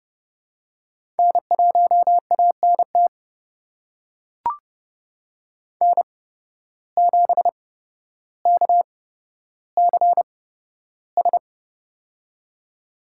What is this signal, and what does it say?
Morse code, heard as a keyed tone, about 32 wpm, spaced as for 13 wpm, 700 Hz. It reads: N1ANT N7KCS
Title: USA Call Signs - 5 Characters - 8x Character Spacing - 30wpm